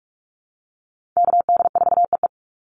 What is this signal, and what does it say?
Morse code, heard as a keyed tone, about 44 wpm, 700 Hz. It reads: KD4EE
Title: Call Signs - 45wpm